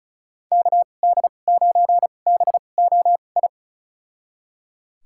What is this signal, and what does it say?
Morse code, heard as a keyed tone, about 34 wpm, spaced as for 25 wpm, 700 Hz. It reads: KD9BOI
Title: USA Call Signs - 6 Characters - 2x Character Spacing - 35wpm